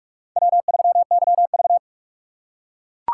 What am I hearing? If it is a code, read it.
Morse code, heard as a keyed tone, about 44 wpm, 700 Hz. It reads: W3YV